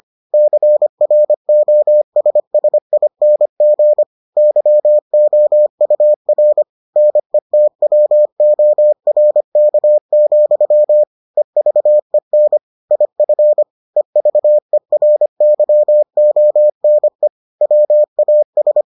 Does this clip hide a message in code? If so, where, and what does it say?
Yes: Morse code from 0.3 to 18.8 s, heard as a keyed tone, about 25 wpm, 600 Hz. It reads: CROSSING YOUR NETWORK, EVEN IF EVERYONE WAS